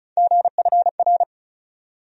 Morse code, heard as a keyed tone, about 34 wpm, 700 Hz. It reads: GFR
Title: USA Call Sign Postfixes - 3 Letters 35wpm